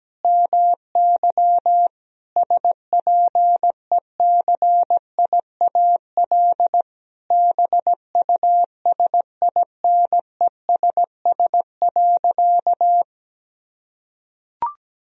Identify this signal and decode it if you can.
Morse code, heard as a keyed tone, about 17 wpm, 700 Hz. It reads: MY SPECIAL BUSINESS.
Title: Sentences from Top 1000 Words - Review Part 3 - 17wpm